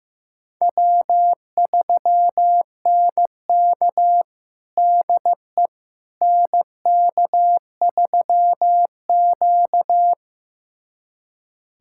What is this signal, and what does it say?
Morse code, heard as a keyed tone, about 15 wpm, 700 Hz. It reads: W3NK DE NK3Q
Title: USA Callsign DE Callsign - 4 Characters - 15wpm